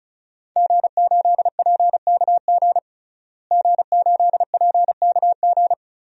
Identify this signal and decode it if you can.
Morse code, heard as a keyed tone, about 36 wpm, 700 Hz. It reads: G8PKG G8PKG
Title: DXCC-100 Call Signs - 5 Characters - Repeated 3x - 35wpm